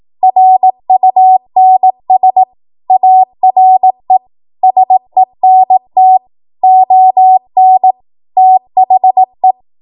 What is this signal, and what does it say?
Morse code, heard as a keyed tone, about 18 wpm, 750 Hz. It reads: RUNS ARE SENT ON THE
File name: Archive / 18 WPM / 18WPM